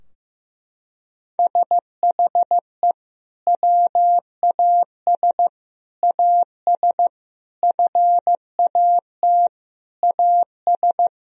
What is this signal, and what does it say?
Morse code, heard as a keyed tone, about 16 wpm, 700 Hz. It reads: SHE WAS AS FAT AS
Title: Sentences from Top 1000 Words - Review Part 4 - 15wpm